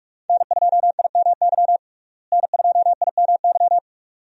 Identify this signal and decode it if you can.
Morse code, heard as a keyed tone, about 45 wpm, 700 Hz. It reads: N2IMY N2IMY
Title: USA Call Signs - 5 Characters - Repeated 2x - 45wpm